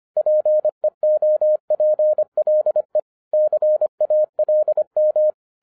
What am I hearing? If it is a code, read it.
Morse code, heard as a keyed tone, about 25 wpm, 610 Hz. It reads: PEOPLE CALM
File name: 2011-07-mp3 / n39 THE 3.11 2 25wpm